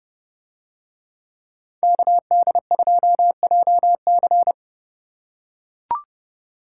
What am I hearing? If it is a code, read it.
Morse code, heard as a keyed tone, about 27 wpm, 700 Hz. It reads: KD2JC